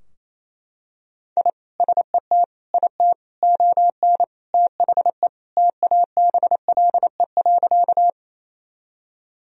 Morse code, heard as a keyed tone, about 28 wpm, 700 Hz. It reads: I SET IT ON THE TABLE.
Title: Sentences from Top 1000 Words - Review Part 2 - 28wpm